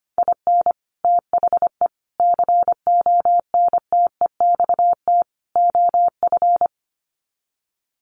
Morse code, heard as a keyed tone, about 25 wpm, 700 Hz. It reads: IN THE CONTEXT OF
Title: Sets of 4 Words 25wpm